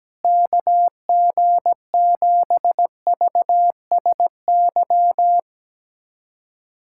Morse code, heard as a keyed tone, about 17 wpm, 700 Hz. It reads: KG7VSY
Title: USA Callsign DE Callsign 17wpm